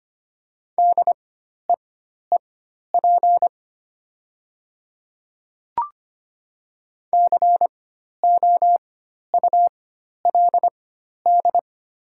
Morse code, heard as a keyed tone, about 25 wpm, spaced as for 12 wpm, 700 Hz. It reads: DEEP COULD
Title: 4x Character Spacing - Top 500 Words - 25wpm